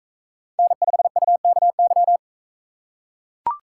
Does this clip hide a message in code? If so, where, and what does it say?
Yes: Morse code from 0.6 to 2.2 s, heard as a keyed tone, about 42 wpm, 700 Hz. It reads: N5UKY